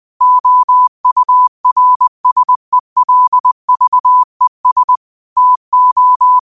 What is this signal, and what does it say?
Morse code, heard as a keyed tone, about 20 wpm, 1000 Hz. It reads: OURSELVES TO
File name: qotdfiles / 20wpm